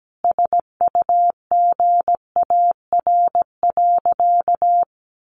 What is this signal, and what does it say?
Morse code, heard as a keyed tone, about 17 wpm, 700 Hz. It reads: SUGAR.
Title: Sentences from Top 1200 Words 17wpm